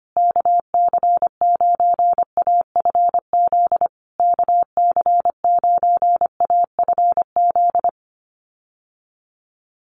Morse code, heard as a keyed tone, about 25 wpm, 700 Hz. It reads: KC9AFZ KC9AFZ
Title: USA Call Signs Repeated 3x 25wpm